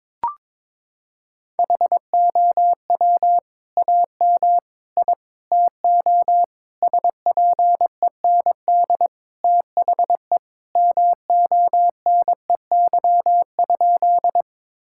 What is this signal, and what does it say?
Morse code, heard as a keyed tone, about 22 wpm, 700 Hz. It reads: HOW AM I TO SPEND THE MONEY?
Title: Sentences from Top 1000 Words - Review Part 5 - 22wpm